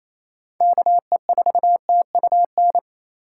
Morse code, heard as a keyed tone, about 28 wpm, 700 Hz. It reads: KE4TUN
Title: USA Call Signs - 6 Characters - 2x Character Spacing - 28wpm